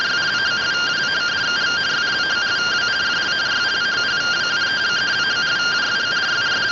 Two-tone teleprinter signal, about 170 Hz shift.
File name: findings / CT AXIAL -256.gen-28 rtty rx